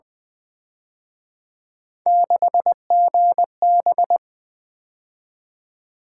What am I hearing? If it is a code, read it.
Morse code, heard as a keyed tone, about 23 wpm, 700 Hz. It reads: 6GB
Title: Speed-Racing - Three Letter-Number Combinations - 20wpm